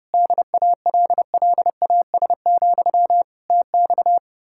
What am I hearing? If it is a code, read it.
Morse code, heard as a keyed tone, about 28 wpm, 700 Hz. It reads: DALLAS, TX